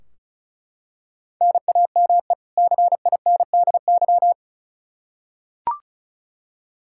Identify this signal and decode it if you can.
Morse code, heard as a keyed tone, about 34 wpm, 700 Hz. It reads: NAME CINDY
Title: K1USN SST QSO Practice 35wpm